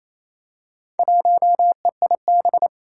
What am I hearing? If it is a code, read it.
Morse code, heard as a keyed tone, about 29 wpm, 700 Hz. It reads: 1EIB
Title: Four Challenging Letter-Number Combinations 28wpm